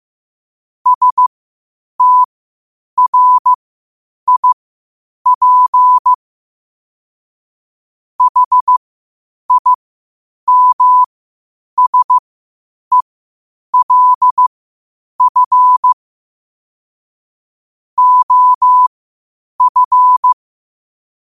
Morse code, heard as a keyed tone, about 15 wpm, spaced as for 8 wpm, 1000 Hz. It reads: STRIP HIMSELF OF
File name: qotdfiles / 8wpm